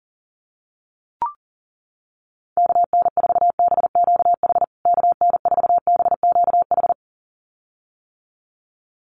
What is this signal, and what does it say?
Morse code, heard as a keyed tone, about 37 wpm, 700 Hz. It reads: KN4BQH KN4BQH
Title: USA Call Signs Repeated 2x 40wpm